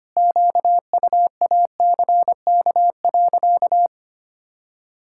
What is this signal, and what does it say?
Morse code, heard as a keyed tone, about 25 wpm, 700 Hz. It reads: QUACK.
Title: Pangram Challenge v2 - 25wpm